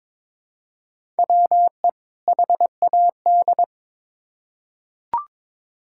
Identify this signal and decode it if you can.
Morse code, heard as a keyed tone, about 22 wpm, 700 Hz. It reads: WE HAD
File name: Build A Phrase - 2-Word Phrases - 22wpm